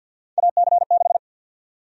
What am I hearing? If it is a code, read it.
Morse code, heard as a keyed tone, about 52 wpm, 700 Hz. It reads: AC6
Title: USA Call Sign Prefixes - Repeated 2x - 50wpm